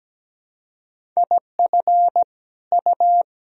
Morse code, heard as a keyed tone, about 17 wpm, 700 Hz. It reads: IF U
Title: Sentences from Top 300 Words and Abbreviations 17wpm